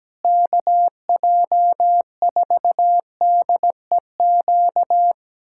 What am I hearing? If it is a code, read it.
Morse code, heard as a keyed tone, about 17 wpm, 700 Hz. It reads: KJ4DEQ
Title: USA Callsign DE Callsign 17wpm